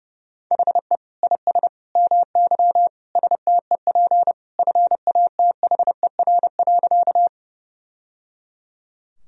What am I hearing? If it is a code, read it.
Morse code, heard as a keyed tone, about 30 wpm, 700 Hz. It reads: HE IS MY STEP FATHER.
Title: Sentences from Top 1000 Words - Review Part 2 - 30wpm